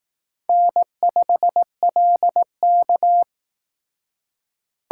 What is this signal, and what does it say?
Morse code, heard as a keyed tone, about 18 wpm, 700 Hz. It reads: N5LK
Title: Speed Racing - USA 1x2 Call Signs 15wpm